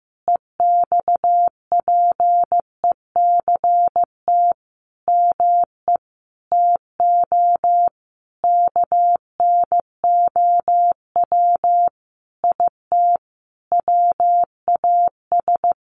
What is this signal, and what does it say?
Morse code, heard as a keyed tone, about 15 wpm, 700 Hz. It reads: EXPECT ME TO KNOW IT WAS
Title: Sentences from Top 1100 Words 15wpm